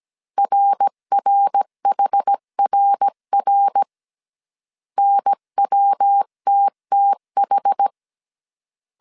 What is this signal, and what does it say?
Morse code, heard as a keyed tone, about 17 wpm, 800 Hz. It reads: RRHRR NWTTH
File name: CW / DET / 1 3 1500C S A 16wpm 2141